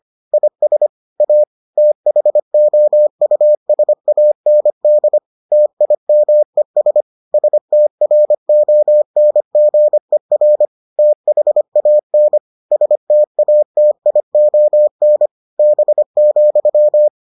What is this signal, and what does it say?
Morse code, heard as a keyed tone, about 25 wpm, 600 Hz. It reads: IS A THOUSAND TIMES STRONGER THAN STATION B,